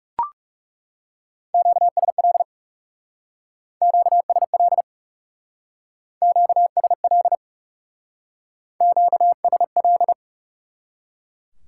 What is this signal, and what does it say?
Morse code, heard as a keyed tone, about 39 wpm, 700 Hz. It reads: QSL QSL QSL QSL